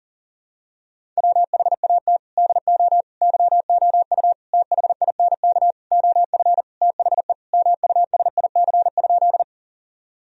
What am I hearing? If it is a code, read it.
Morse code, heard as a keyed tone, about 40 wpm, 700 Hz. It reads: WHAT DO YOU THINK OF THE MUSIC?
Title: Sentences from Top 1000 Words - Review Part 1 - 40wpm